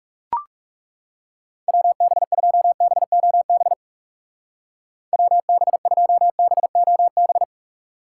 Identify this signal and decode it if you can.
Morse code, heard as a keyed tone, about 35 wpm, 700 Hz. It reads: WB2BOB WB2BOB
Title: Speed Racing - USA 2x3 Call Signs 30wpm